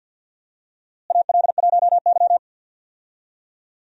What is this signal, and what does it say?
Morse code, heard as a keyed tone, about 51 wpm, 700 Hz. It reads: AL1Y
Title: USA Call Signs - 4 Characters - Repeated 3x - 50wpm